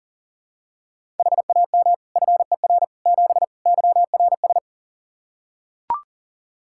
Morse code, heard as a keyed tone, about 37 wpm, 700 Hz. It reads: HAM FER 7 YRS
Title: Practice QSO - Exchange 3 - More Info - 40wpm